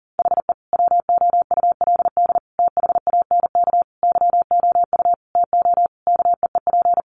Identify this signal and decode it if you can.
Morse code, heard as a keyed tone, about 40 wpm, 700 Hz. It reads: HE WOULD THANK YOU TO KEEP
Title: Sentences from Top 1100 Words 40wpm